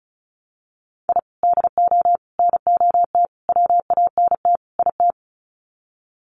Morse code, heard as a keyed tone, about 34 wpm, 700 Hz. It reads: I DO NOT WANT IT
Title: Sets of 5 Words Encore2 35wpm